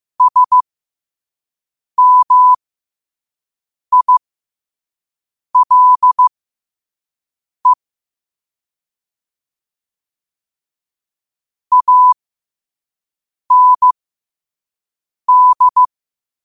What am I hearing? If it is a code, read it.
Morse code, heard as a keyed tone, about 15 wpm, spaced as for 5 wpm, 1000 Hz. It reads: SMILE AND
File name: qotdfiles / 5wpm